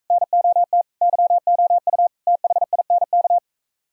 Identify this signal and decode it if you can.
Morse code, heard as a keyed tone, about 43 wpm, 700 Hz. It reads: NOT YOU THINK